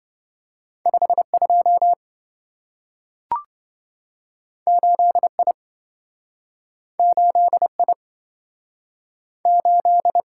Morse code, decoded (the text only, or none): 52 8I 8I 8